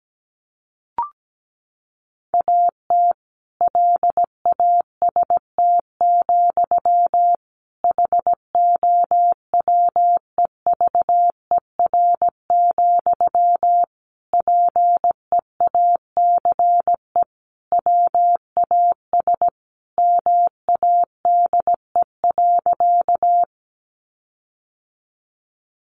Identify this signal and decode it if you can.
Morse code, heard as a keyed tone, about 17 wpm, 700 Hz. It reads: AT LAST, HOWEVER, PEACE WAS MADE.
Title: Sentences from Top 1500 Words 17wpm